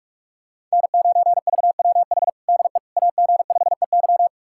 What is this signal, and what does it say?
Morse code, heard as a keyed tone, about 46 wpm, 700 Hz. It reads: N9VWH DE AG5EY
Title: USA Callsign DE Callsign - 5 Characters - 45wpm